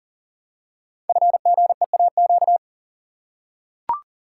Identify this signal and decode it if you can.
Morse code, heard as a keyed tone, about 41 wpm, 700 Hz. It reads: FGEAQ
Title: Code Groups - 5 Letters - 40wpm